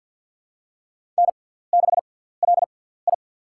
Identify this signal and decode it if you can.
Morse code, heard as a keyed tone, about 47 wpm, spaced as for 17 wpm, 700 Hz. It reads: N6LI
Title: USA Call Signs - 6x Character Spacing - 50wpm